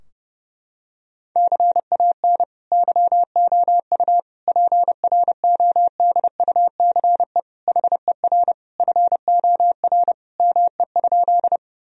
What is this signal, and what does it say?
Morse code, heard as a keyed tone, about 32 wpm, 700 Hz. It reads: CAN YOU PRODUCE HER FOR ME?
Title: Sentences from Top 1000 Words - Review Part 3 - 30wpm